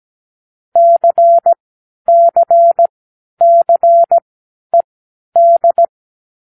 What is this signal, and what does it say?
Morse code, heard as a keyed tone, about 17 wpm, spaced as for 10 wpm, 680 Hz. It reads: CCCED
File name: k7qo / Lesson9(A-F)